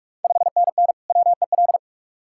Morse code, heard as a keyed tone, about 44 wpm, 690 Hz. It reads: 5NN WEL
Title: Ontario QSO Party Exchange 45wpm